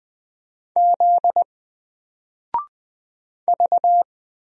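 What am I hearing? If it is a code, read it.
Morse code, heard as a keyed tone, about 20 wpm, 700 Hz. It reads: Z V
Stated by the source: Single letter and number Morse drill at 20 wpm, from N4LCD.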